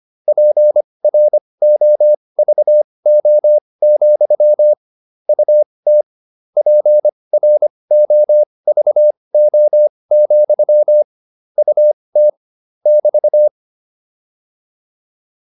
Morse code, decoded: PROVO, UT PROVO, UT <BT>

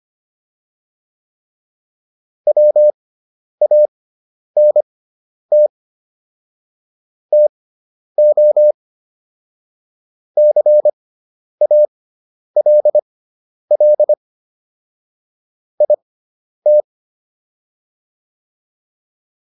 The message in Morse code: WANT TO CALL IT

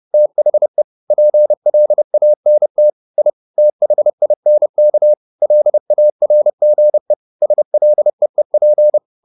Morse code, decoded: THE PLANT I THINK LARGE SLEEP

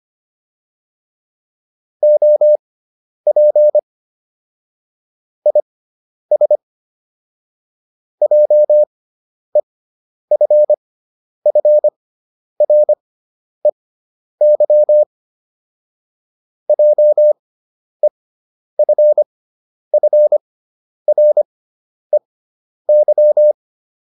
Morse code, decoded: OP IS JEFFREY JEFFREY

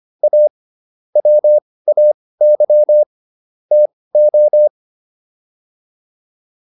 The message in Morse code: A WAY TO

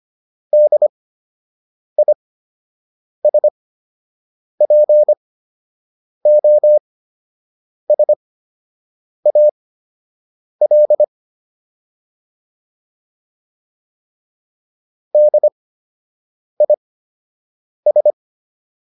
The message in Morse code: DISPOSAL DIS

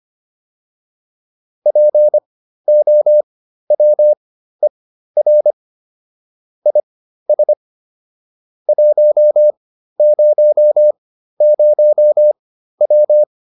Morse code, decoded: POWER IS 100W

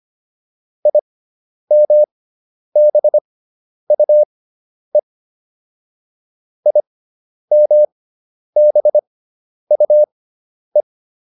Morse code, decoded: IMBUE IMBUE

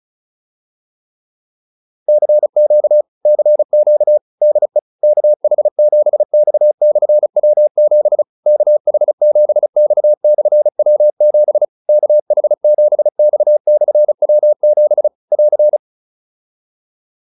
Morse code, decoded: CQ CQ DE KH7X/W7 KH7X/W7 KH7X/W7 <AR>

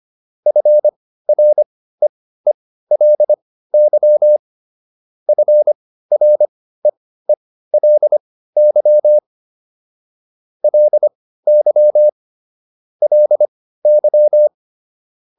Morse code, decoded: FREELY FREELY LY LY

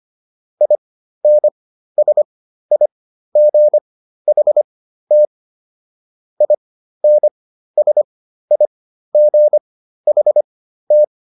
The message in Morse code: INSIGHT INSIGHT

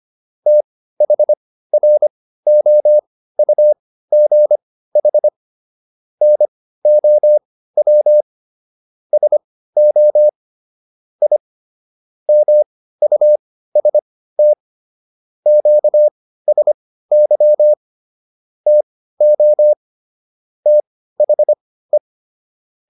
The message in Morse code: THROUGH NOW SO I MUST QSY TO THE

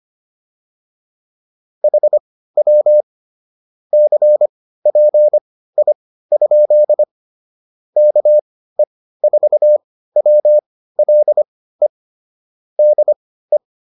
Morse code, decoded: HW CPI? KE4WLE DE

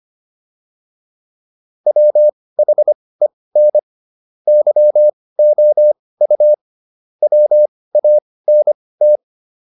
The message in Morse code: WHEN YOU WANT

